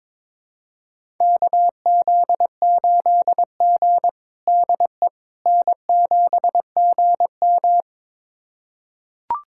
Morse code, heard as a keyed tone, about 22 wpm, 700 Hz. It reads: KZ8G DE N7GM
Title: USA Callsign DE Callsign - 4 Characters - 22wpm